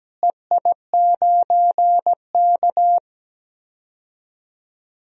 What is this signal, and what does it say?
Morse code, heard as a keyed tone, about 17 wpm, 700 Hz. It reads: EI9K
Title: DXCC-100 Call Signs - 4 Characters - Repeated 3x - 17wpm